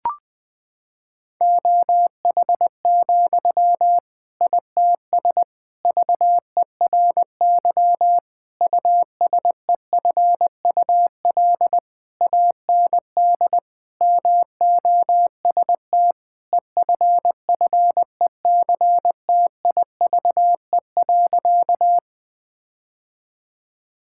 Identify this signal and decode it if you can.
Morse code, heard as a keyed tone, about 21 wpm, 700 Hz. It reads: OH, ITS VERY USEFUL AND MOST EFFECTIVE.
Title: Sentences from Top 1000 Words - Review Part 6 - 20wpm